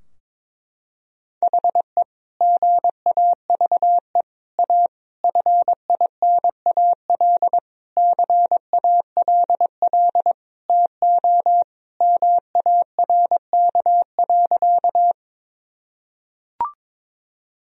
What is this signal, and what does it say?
Morse code, heard as a keyed tone, about 22 wpm, 700 Hz. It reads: HE GAVE A FINAL CALL TO MARK.
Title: Sentences from Top 1000 Words - Review Part 2 - 22wpm